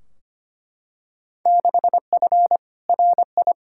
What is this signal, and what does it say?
Morse code, heard as a keyed tone, about 25 wpm, 700 Hz. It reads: 6F RI
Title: Field Day Exchanges Encore - 25wpm